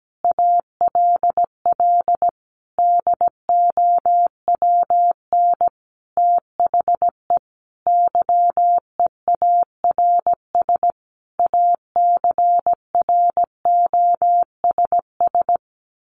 Morse code, heard as a keyed tone, about 17 wpm, 700 Hz. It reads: ALL DOWN THE YEARS ACROSS
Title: A Sound Of Thunder 17wpm